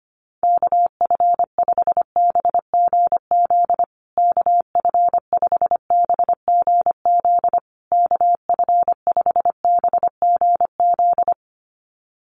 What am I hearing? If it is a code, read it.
Morse code, heard as a keyed tone, about 25 wpm, 700 Hz. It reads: KF5BGZ KF5BGZ KF5BGZ
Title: USA Call Signs Repeated 3x 25wpm